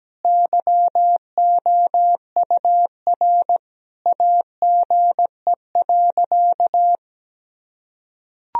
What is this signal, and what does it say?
Morse code, heard as a keyed tone, about 17 wpm, 700 Hz. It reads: YOUR AGE.
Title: Sentences from Top 1000 Words - Review Part 3 - 17wpm